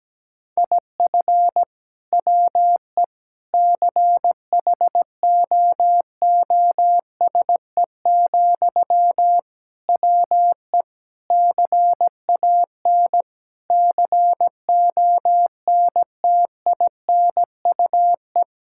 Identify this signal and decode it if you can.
Morse code, heard as a keyed tone, about 17 wpm, 700 Hz. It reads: IF WE CHOOSE, WE CAN CONTINUE
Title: Sentences from Top 1000 Words - Review Part 6 - 17wpm